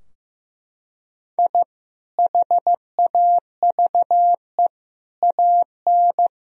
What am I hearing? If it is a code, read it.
Morse code, heard as a keyed tone, about 15 wpm, 700 Hz. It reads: I HAVE AN
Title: Sentences from Top 1000 Words - Review Part 2 - 15wpm